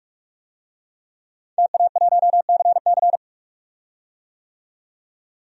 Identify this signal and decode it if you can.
Morse code, heard as a keyed tone, about 46 wpm, 700 Hz. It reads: TA1CC
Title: DXCC-100 Call Signs - 5 Characters - 45wpm